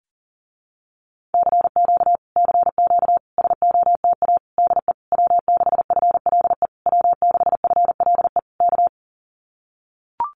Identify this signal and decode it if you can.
Morse code, heard as a keyed tone, about 40 wpm, 700 Hz. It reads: CQ CQ SOTA DE W6FLE W6FLE K
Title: Common QSO Phrases Encore 40wpm